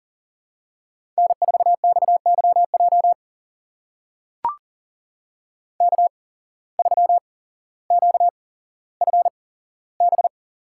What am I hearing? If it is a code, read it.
Morse code, heard as a keyed tone, about 38 wpm, spaced as for 17 wpm, 700 Hz. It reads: N4XYJ K3QFB
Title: USA Call Signs - 5 Characters - 8x Character Spacing - 40wpm